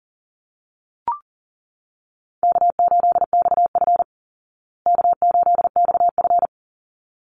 Morse code, heard as a keyed tone, about 40 wpm, 700 Hz. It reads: K8XF K8XF
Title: Easy Call Signs Repeated 2x - 40wpm